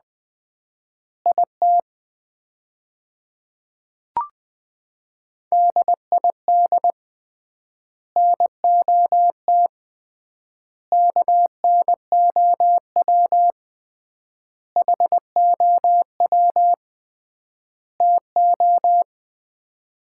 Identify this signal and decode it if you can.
Morse code, heard as a keyed tone, about 20 wpm, 700 Hz. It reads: IT DID NOT KNOW HOW TO
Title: 3x Word-Spacing - Sets of 5 Words - 20wpm